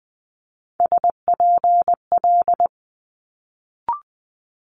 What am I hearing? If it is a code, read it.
Morse code, heard as a keyed tone, about 20 wpm, 700 Hz. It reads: SPL
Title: Multi-letter Phonemes Quick Fire 20wpm